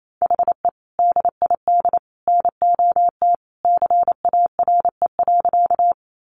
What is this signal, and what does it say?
Morse code, heard as a keyed tone, about 28 wpm, 700 Hz. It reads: HE DID NOT CARE.
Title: Sentences from Top 600 Words 28wpm